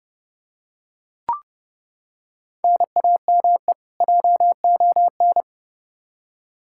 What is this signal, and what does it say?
Morse code, heard as a keyed tone, about 30 wpm, 700 Hz. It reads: NAME JON
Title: K1USN SST QSO Practice 30wpm